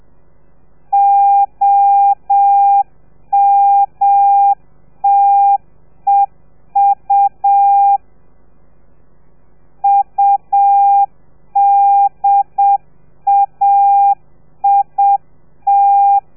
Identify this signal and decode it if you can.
Morse code, heard as a keyed tone, about 7 wpm, 800 Hz. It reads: OMTEU UDAIT